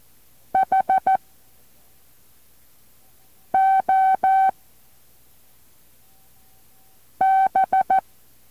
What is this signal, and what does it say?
Morse code, heard as a keyed tone, about 14 wpm, 770 Hz. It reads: H O B
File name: intl morse code2